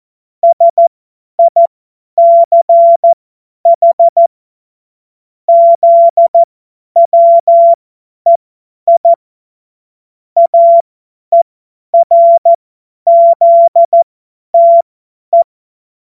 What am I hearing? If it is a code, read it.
Morse code, heard as a keyed tone, about 14 wpm, spaced as for 10 wpm, 680 Hz. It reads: SICH ZWEI AERZTE
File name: Morsekurs / Witze / Witz 26 März 2024 50 F70